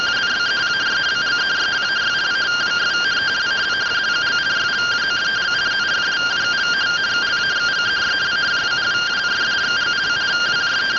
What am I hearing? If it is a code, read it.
Two-tone teleprinter signal, about 170 Hz shift.